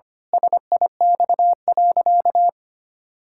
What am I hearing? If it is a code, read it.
Morse code, heard as a keyed tone, about 25 wpm, 700 Hz. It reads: SIX.
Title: Pangram Challenge v2 - 25wpm